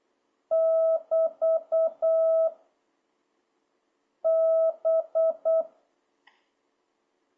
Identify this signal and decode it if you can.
Morse code, heard as a keyed tone, about 8 wpm, 650 Hz. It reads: <BT> B
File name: jh8cbh / 2026cw